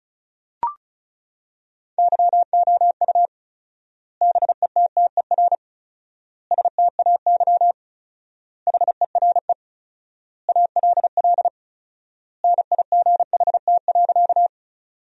Morse code, heard as a keyed tone, about 36 wpm, 700 Hz. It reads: YOU BETTER STAY HERE ALL NIGHT.
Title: 4x Word-Spacing - Sentences from Top 500 Words - 35wpm